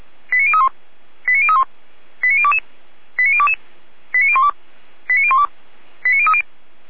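Five-tone selective call (ZVEI), heard as quick runs of five tones, tones of about 70 ms. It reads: 89031 89031 89022 89022 89012 89012 89030
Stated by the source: ZVEI five-tone call received on SDR, from the audio archive of the Orlamünde fire brigade (feuerwehr-orlamuende).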